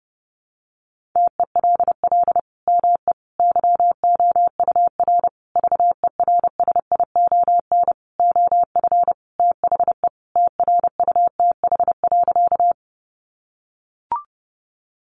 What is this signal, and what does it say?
Morse code, heard as a keyed tone, about 30 wpm, 700 Hz. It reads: TELL ME YOUR VERSION OF THE TRUTH.
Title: Sentences from Top 1400 Words 30wpm